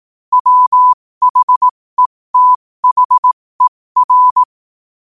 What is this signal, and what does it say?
Morse code, heard as a keyed tone, about 18 wpm, spaced as for 15 wpm, 1000 Hz. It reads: WHETHER
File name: qotdfiles / 15wpm